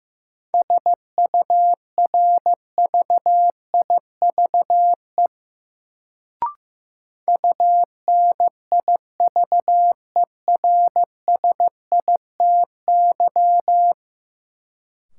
Morse code, heard as a keyed tone, about 15 wpm, 700 Hz. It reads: SURVIVE UNIVERSITY
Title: Sentences from Top 1300 Words - New Word Drill Alt - 15wpm